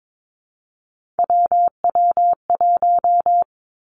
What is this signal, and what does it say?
Morse code, heard as a keyed tone, about 22 wpm, 700 Hz. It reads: WW1
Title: Three Challenging Letter-Number Combinations 22wpm